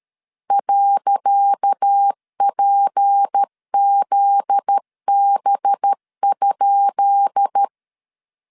Morse code, heard as a keyed tone, about 13 wpm, 800 Hz. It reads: .PZB?
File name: LEZIONI / 10 SIGNS / 10C DOT 12wpm 2451